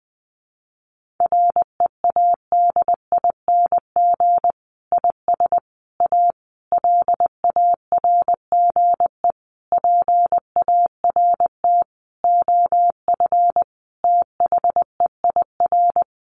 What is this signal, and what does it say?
Morse code, heard as a keyed tone, about 20 wpm, 700 Hz. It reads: READING IS A LARGE PART OF THEIR
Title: Sentences from Top 1400 Words 20wpm